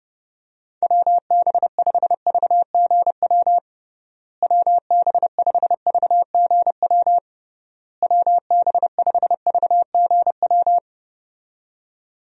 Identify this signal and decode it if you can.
Morse code, heard as a keyed tone, about 32 wpm, 700 Hz. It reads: WB5VGW WB5VGW WB5VGW
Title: USA Call Signs - 6 Characters - Repeated 3x - 30wpm